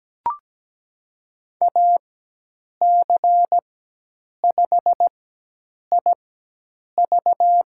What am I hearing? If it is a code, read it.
Morse code, heard as a keyed tone, about 17 wpm, spaced as for 8 wpm, 700 Hz. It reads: AC5IV
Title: USA Call Signs - 5 Characters - 4x Character Spacing - 17wpm